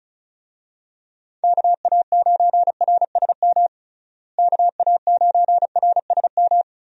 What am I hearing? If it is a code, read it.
Morse code, heard as a keyed tone, about 34 wpm, 700 Hz. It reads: KA9RSM KA9RSM
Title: USA Call Signs - 6 Characters - Repeated 3x - 35wpm